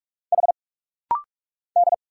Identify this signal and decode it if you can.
Morse code, heard as a keyed tone, about 46 wpm, 690 Hz. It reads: H D
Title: Single Letters - Rapid-Fire 45wpm